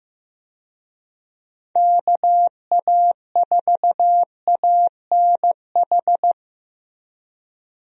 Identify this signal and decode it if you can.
Morse code, heard as a keyed tone, about 15 wpm, 700 Hz. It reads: KA4ANH
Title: USA Call Signs - 6 Characters - Repeated 2x - 15wpm